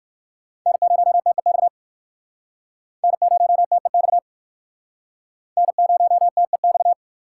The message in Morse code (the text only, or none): N0TEX N0TEX N0TEX